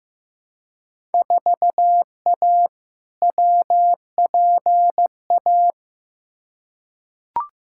Morse code, heard as a keyed tone, about 15 wpm, 700 Hz. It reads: 4A WPA
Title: Field Day Exchanges Encore - 15wpm